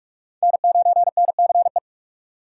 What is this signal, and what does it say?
Morse code, heard as a keyed tone, about 46 wpm, 700 Hz. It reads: N9NCE